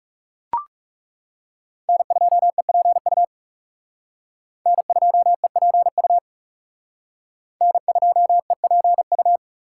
Morse code, heard as a keyed tone, about 36 wpm, 700 Hz. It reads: N2EPU N2EPU N2EPU
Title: Speed Racing - USA 1x3 Call Signs 30wpm